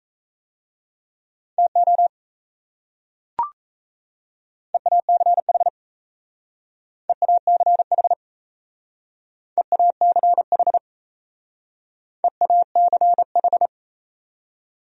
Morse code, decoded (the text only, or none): TO EACH EACH EACH EACH